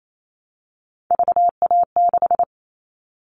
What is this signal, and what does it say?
Morse code, heard as a keyed tone, about 28 wpm, 700 Hz. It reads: VA6
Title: Callsign Prefixes for Top 25 DXCC - 28wpm